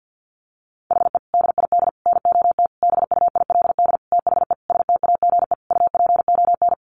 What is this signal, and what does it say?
Morse code, heard as a keyed tone, about 50 wpm, 700 Hz. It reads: HE DID NOT BUILD THE STAGE UPON
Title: Sentences from Top 1200 Words 50wpm